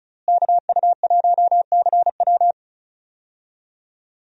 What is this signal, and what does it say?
Morse code, heard as a keyed tone, about 36 wpm, 700 Hz. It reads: KU1CW
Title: USA Call Signs - 5 Characters - Repeated 3x - 35wpm